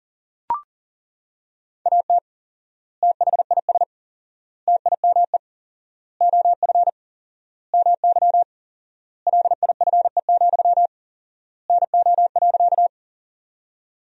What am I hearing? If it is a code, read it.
Morse code, heard as a keyed tone, about 40 wpm, 700 Hz. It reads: AT THIS TIME OF MY LIFE, NO.
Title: 4x Word-Spacing - Sentences from Top 300 Words - 40wpm